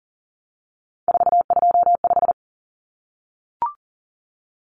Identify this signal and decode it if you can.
Morse code, heard as a keyed tone, about 40 wpm, 700 Hz. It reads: 425
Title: Numbers - 3-Digit 40wpm